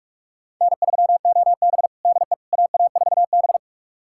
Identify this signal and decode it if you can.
Morse code, heard as a keyed tone, about 46 wpm, 700 Hz. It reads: N3OB DE AA4B
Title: USA Callsign DE Callsign - 4 Characters - 45wpm